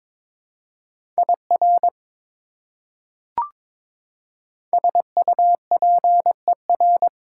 Morse code, heard as a keyed tone, about 22 wpm, 700 Hz. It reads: IR SUPER
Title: Prefixes and Suffixes 22wpm